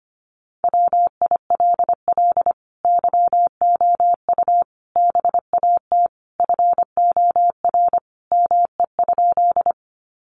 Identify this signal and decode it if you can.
Morse code, heard as a keyed tone, about 25 wpm, 700 Hz. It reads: WILL YOU BAT FOR ME?
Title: Sentences from Top 1500 Words 25wpm